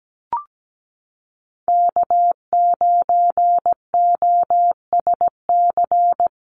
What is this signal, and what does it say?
Morse code, heard as a keyed tone, about 17 wpm, 700 Hz. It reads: K9OSC
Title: NA SOTA Chaser Callsigns 17wpm